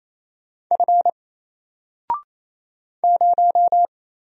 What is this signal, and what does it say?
Morse code, heard as a keyed tone, about 27 wpm, 700 Hz. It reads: F 0
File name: Single-Letter-Number-28wpm